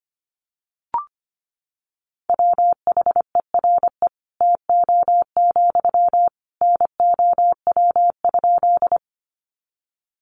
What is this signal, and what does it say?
Morse code, heard as a keyed tone, about 24 wpm, 700 Hz. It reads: WHERE TO, NOW?
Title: Sentences from Top 200 Words 25wpm